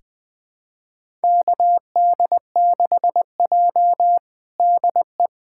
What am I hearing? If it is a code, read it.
Morse code, heard as a keyed tone, about 20 wpm, 700 Hz. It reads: KD6J DE
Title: USA Callsign DE Callsign - 4 Characters - 20wpm